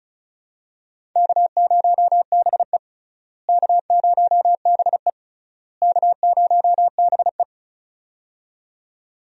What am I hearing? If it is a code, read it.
Morse code, heard as a keyed tone, about 34 wpm, 700 Hz. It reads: K0BE K0BE K0BE
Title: USA Call Signs - 4 Characters - Repeated 3x - 35wpm